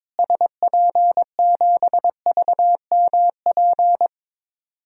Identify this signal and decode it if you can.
Morse code, heard as a keyed tone, about 22 wpm, 700 Hz. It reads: SP7VMP
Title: DXCC-100 Call Signs - 6 Characters - Repeated 3x - 22wpm